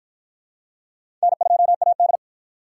Morse code, heard as a keyed tone, about 53 wpm, 700 Hz. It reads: N2AD